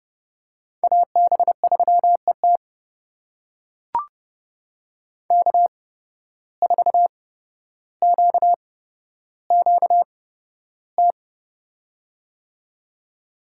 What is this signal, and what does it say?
Morse code, heard as a keyed tone, about 30 wpm, spaced as for 13 wpm, 700 Hz. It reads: AB3ET K4QQT
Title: USA Call Signs - 5 Characters - 8x Character Spacing - 30wpm